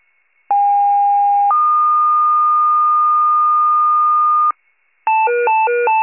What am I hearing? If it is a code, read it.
Two-tone page, heard as a one-second tone, then a three-second tone, about 810 then 1220 Hz. It is heard from 0.5 to 4.5 s.
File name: pager / 46 RESCUE / wcalert